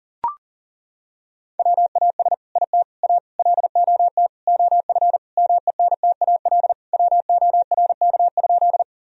Morse code, heard as a keyed tone, about 37 wpm, 700 Hz. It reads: WAS IT A LOT OF MENTAL WORK?
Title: Sentences from Top 1000 Words - Review Part 4 - 40wpm